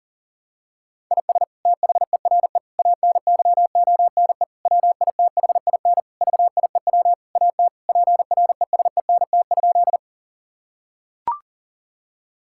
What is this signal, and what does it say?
Morse code, heard as a keyed tone, about 40 wpm, 700 Hz. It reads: IS THERE ANYONE WITHIN VIEW AT PRESENT?
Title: Sentences from Top 1000 Words - Review Part 6 - 40wpm